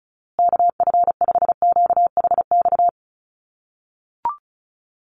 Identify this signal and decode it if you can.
Morse code, heard as a keyed tone, about 34 wpm, 700 Hz. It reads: KF5QHX
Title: USA Call Signs Repeated 2x 35wpm